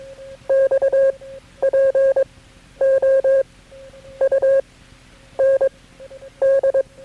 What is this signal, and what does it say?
Morse code, heard as a keyed tone, about 21 wpm, spaced as for 9 wpm, 540 Hz. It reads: X POUND
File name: KB6MT-tape3-side1